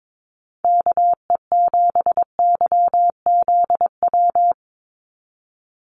Morse code, heard as a keyed tone, about 22 wpm, 700 Hz. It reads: KE7YZW
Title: USA Call Signs 22wpm